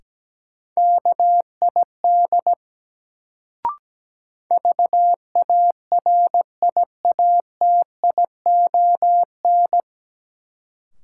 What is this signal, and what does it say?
Morse code, heard as a keyed tone, about 17 wpm, 700 Hz. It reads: KID VARIATION
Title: Sentences from Top 1600 Words - New Word Drill Alt - 17wpm